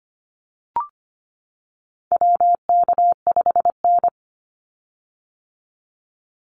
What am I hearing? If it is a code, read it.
Morse code, heard as a keyed tone, about 25 wpm, 700 Hz. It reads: WK5N
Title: Call Signs - 25wpm